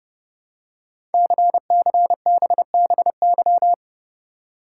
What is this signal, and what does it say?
Morse code, heard as a keyed tone, about 30 wpm, 700 Hz. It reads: CCBBY
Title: Code Groups - 5 Letters - 30wpm